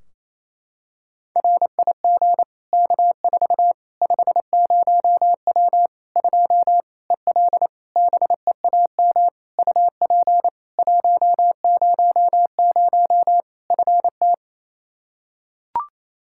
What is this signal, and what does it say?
Morse code, heard as a keyed tone, about 27 wpm, 700 Hz. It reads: RIG K4 50W 2 EL BEAM UP 100 FT
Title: Practice QSO - Exchange 2 - Info - 28wpm